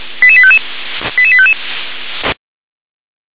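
Five-tone selective call (ZVEI), heard as quick runs of five tones, tones of about 70 ms. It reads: 88855 88855
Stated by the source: ZVEI five-tone selective call received by SDR.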